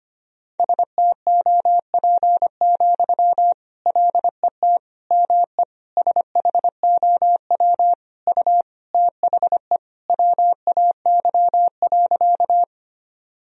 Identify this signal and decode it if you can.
Morse code, heard as a keyed tone, about 25 wpm, 700 Hz. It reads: STOP, LET ME SHOW U THE WAY.
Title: Sentences from Top 300 Words and Abbreviations 25wpm